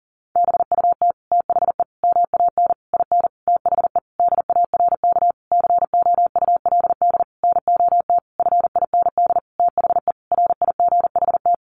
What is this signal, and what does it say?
Morse code, heard as a keyed tone, about 40 wpm, 700 Hz. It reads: BUT THE MAN IN THE DARK COULD NOT FIND THE RIGHT
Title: Sentences from Top 600 Words 40wpm